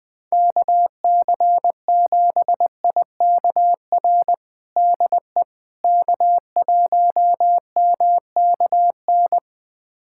Morse code, heard as a keyed tone, about 21 wpm, 700 Hz. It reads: KC7IKR DE K1MKN
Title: USA Callsign DE Callsign 20wpm